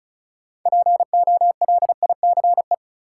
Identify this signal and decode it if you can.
Morse code, heard as a keyed tone, about 36 wpm, 700 Hz. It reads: POLICE
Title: Six Letter Words - 35wpm